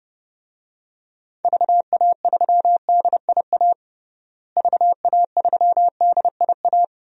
Morse code, heard as a keyed tone, about 30 wpm, 700 Hz. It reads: VA3DIA VA3DIA
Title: DXCC-100 Call Signs - 6 Characters - Repeated 2x - 30wpm